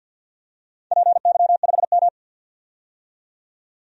Morse code, heard as a keyed tone, about 53 wpm, 700 Hz. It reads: PY5M
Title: DXCC-100 Call Signs - 4 Characters - 50wpm